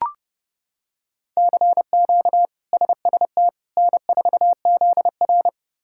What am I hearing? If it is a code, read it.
Morse code, heard as a keyed tone, about 32 wpm, 700 Hz. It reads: CQ SST N4ZR
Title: K1USN SST QSO Practice - 30wpm